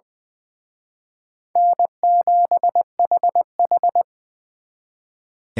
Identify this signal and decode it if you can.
Morse code, heard as a keyed tone, about 23 wpm, 700 Hz. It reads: N7HH
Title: Speed Racing - USA 1x2 Call Signs 20wpm